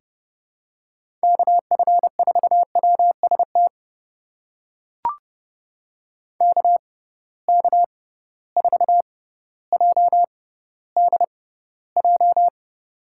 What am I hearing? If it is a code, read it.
Morse code, heard as a keyed tone, about 32 wpm, spaced as for 16 wpm, 700 Hz. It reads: KF4WST KK4JDJ
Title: USA Call Signs - 6 Characters - 6x Character Spacing - 30wpm